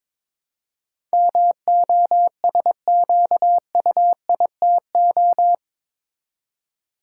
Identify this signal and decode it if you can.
Morse code, heard as a keyed tone, about 22 wpm, 700 Hz. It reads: MOSQUITO
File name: practice / 22wpm / Eight Letter Words - 22wpm